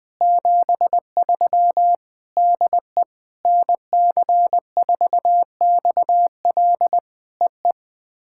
Morse code, heard as a keyed tone, about 21 wpm, 700 Hz. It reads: 73 DE NC4XL EE
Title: POTA QSO Practice - 20wpm